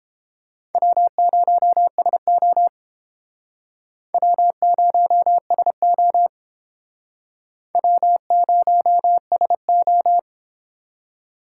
Morse code, heard as a keyed tone, about 29 wpm, 700 Hz. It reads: W0SO W0SO W0SO